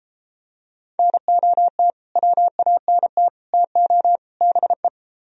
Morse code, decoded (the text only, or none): NOT WANT TO BE